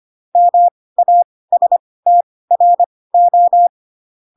Morse code, heard as a keyed tone, about 25 wpm, spaced as for 18 wpm, 700 Hz. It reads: MASTRO
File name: Pinocchio / Pinocchio ch1 12 25 ik1hge